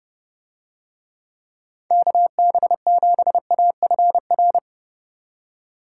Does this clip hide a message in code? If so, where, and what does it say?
Yes: Morse code from 1.9 to 4.6 s, heard as a keyed tone, about 27 wpm, 700 Hz. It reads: KB7AFR